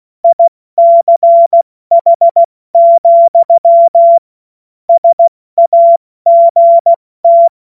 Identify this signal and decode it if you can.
Morse code, heard as a keyed tone, about 16 wpm, 680 Hz. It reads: ICH, SAGT
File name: Morsekurs / Witze / Witz 04. Januar 2022, 70 F80